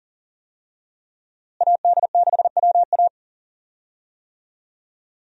Morse code, heard as a keyed tone, about 40 wpm, 700 Hz. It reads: AD6WA
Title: USA Call Signs - 5 Characters - Repeated 3x - 40wpm